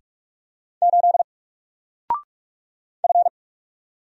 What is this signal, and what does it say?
Morse code, heard as a keyed tone, about 44 wpm, 700 Hz. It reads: 8 F
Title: Single Character (Letter:Number:Punctuation:Prosign) 45wpm